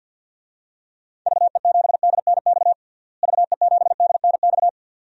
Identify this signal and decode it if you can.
Morse code, heard as a keyed tone, about 50 wpm, 700 Hz. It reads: VE7DNX VE7DNX
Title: DXCC-100 Call Signs - 6 Characters - Repeated 2x - 50wpm